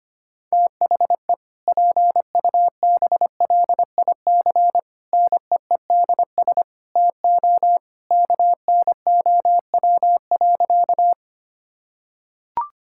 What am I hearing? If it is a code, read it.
Morse code, heard as a keyed tone, about 25 wpm, 700 Hz. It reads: THE PUBLIC NEEDS TO KNOW.
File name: practice / 25wpm / Sentences with 2-Word Phrases - 25wpm